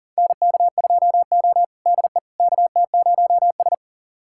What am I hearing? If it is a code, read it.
Morse code, heard as a keyed tone, about 40 wpm, 700 Hz. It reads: NK2O DE KT0S
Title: USA Callsign DE Callsign - 4 Characters - 40wpm